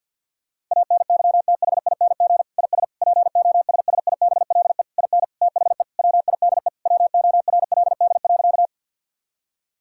Morse code, heard as a keyed tone, about 52 wpm, 700 Hz. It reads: ANYTHING IS POSSIBLE IN THE WIDE WORLD.
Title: Sentences from Top 1000 Words - Review Part 6 - 50wpm